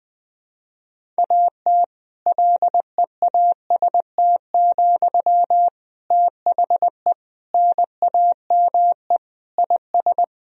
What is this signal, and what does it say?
Morse code, heard as a keyed tone, about 20 wpm, 700 Hz. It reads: AT LEAST, THE NAME IS
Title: Sentences from Top 1000 Words - Review Part 4 - 20wpm